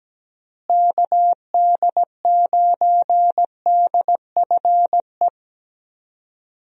Morse code, heard as a keyed tone, about 17 wpm, 700 Hz. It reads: KD9DFE
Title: USA Call Signs - 6 Characters - Repeated 2x - 17wpm